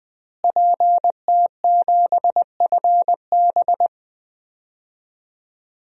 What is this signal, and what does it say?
Morse code, heard as a keyed tone, about 21 wpm, 700 Hz. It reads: PT7FB
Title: DXCC-100 Call Signs - 5 Characters - Repeated 2x - 20wpm